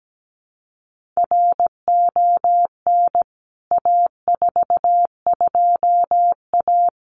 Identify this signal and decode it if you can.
Morse code, heard as a keyed tone, about 17 wpm, 700 Hz. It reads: RON A42A
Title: CWT Exchange 17wpm